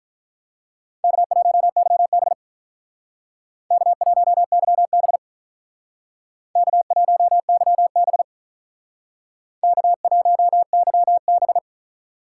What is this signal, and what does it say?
Morse code, heard as a keyed tone, about 45 wpm, 700 Hz. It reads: K1YB K1YB K1YB K1YB